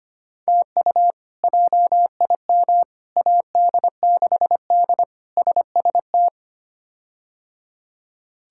Morse code, heard as a keyed tone, about 25 wpm, 700 Hz. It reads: TU JIM AD6D SST
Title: K1USN SST QSO Practice - 25wpm